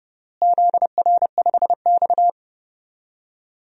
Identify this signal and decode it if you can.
Morse code, heard as a keyed tone, about 30 wpm, 700 Hz. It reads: ZR5X
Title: DXCC-100 Call Signs - 4 Characters - Repeated 3x - 30wpm